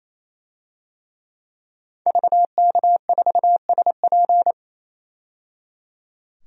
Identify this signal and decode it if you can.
Morse code, heard as a keyed tone, about 28 wpm, 700 Hz. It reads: VK4SP
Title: DXCC-100 Call Signs - 5 Characters - 28wpm